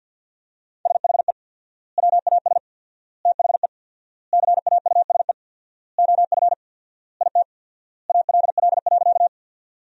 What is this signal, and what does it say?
Morse code, heard as a keyed tone, about 50 wpm, 700 Hz. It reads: SHE WAS THE CAUSE OF IT ALL.
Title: 4x Word-Spacing - Sentences from Top 200 Words - 50wpm